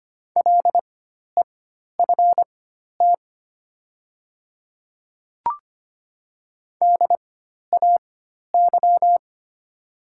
Morse code, heard as a keyed tone, about 25 wpm, spaced as for 12 wpm, 700 Hz. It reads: LEFT DAY
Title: 4x Character Spacing - Top 300 Words - 25wpm